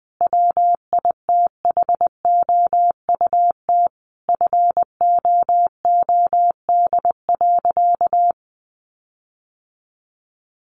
Morse code, heard as a keyed tone, about 20 wpm, 700 Hz. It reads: WITHOUT FOOD.
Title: Sentences from Top 1500 Words 20wpm